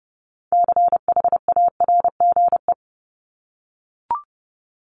Morse code, heard as a keyed tone, about 30 wpm, 700 Hz. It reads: CHARGE
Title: Top 751-1000 Words 30wpm